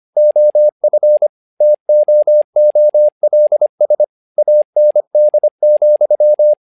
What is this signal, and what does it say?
Morse code, heard as a keyed tone, about 25 wpm, 600 Hz. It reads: OF TOOLS AND,
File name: learning center / tower trouble / tower trouble 25wpm